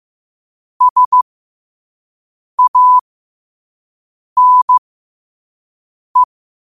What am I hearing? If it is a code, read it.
Morse code, heard as a keyed tone, about 15 wpm, spaced as for 5 wpm, 1000 Hz. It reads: SANE